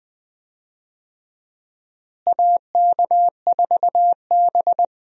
Morse code, heard as a keyed tone, about 20 wpm, 700 Hz. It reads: AK4B
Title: USA Call Signs - 4 Characters - Repeated 3x - 20wpm